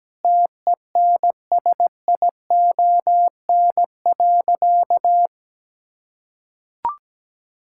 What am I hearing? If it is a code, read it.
Morse code, heard as a keyed tone, about 17 wpm, 700 Hz. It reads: TENSION.
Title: Sentences from Top 1600 Words 17wpm